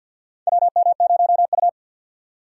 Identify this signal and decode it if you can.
Morse code, heard as a keyed tone, about 52 wpm, 700 Hz. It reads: WM0U
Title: USA Call Signs - 4 Characters - 3x Character Spacing - 50wpm